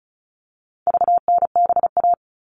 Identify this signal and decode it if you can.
Morse code, heard as a keyed tone, about 34 wpm, 700 Hz. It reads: VNBA
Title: All Cut Numbers - 4-Digit 35wpm